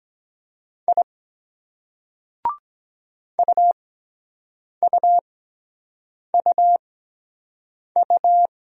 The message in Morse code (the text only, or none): I UUUU